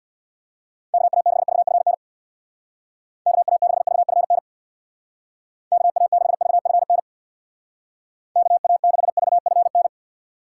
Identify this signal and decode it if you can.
Morse code, heard as a keyed tone, about 65 wpm, 700 Hz. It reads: KA6VFN KA6VFN KA6VFN KA6VFN